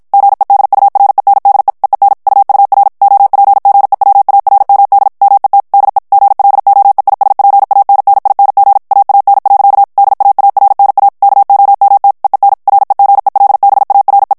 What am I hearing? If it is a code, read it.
Morse code, heard as a keyed tone, about 56 wpm, 800 Hz. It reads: GEDURENDE EEN UUR OPGEWARMD MET DE GLOEISPANNING AAN. DAARNA KOMT EEN REGELBARE
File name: HOLLANDAIS texte clair 2 vitesse 50 mots minute